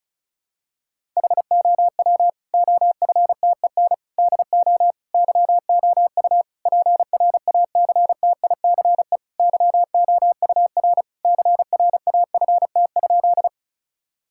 Morse code, heard as a keyed tone, about 34 wpm, 700 Hz. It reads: HOW OFTEN DO YOU PRACTICE YOUR CRAFT?
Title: Sentences from Top 1000 Words - Review Part 6 - 35wpm